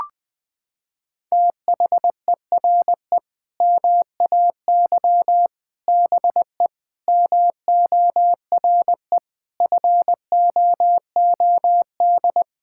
Morse code, heard as a keyed tone, about 20 wpm, 700 Hz. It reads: THERE MAY BE MORE FOOD
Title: Sentences from Top 1000 Words - Review Part 6 - 20wpm